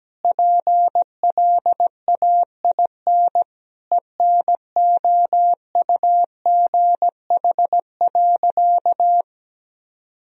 Morse code, heard as a keyed tone, about 17 wpm, 700 Hz. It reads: PLAIN ENOUGH.
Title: Sentences from Top 1000 Words - Review Part 2 - 17wpm